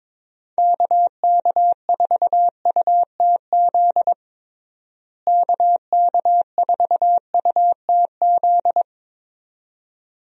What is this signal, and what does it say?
Morse code, heard as a keyed tone, about 22 wpm, 700 Hz. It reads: KK4UTZ KK4UTZ
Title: USA Call Signs - 6 Characters - Repeated 2x - 22wpm